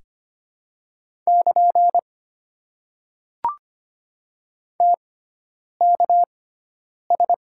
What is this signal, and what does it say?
Morse code, heard as a keyed tone, about 25 wpm, spaced as for 9 wpm, 700 Hz. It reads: <KN> TKS
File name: Top 90 QSO Elements - 6x Character Spacing - 25wpm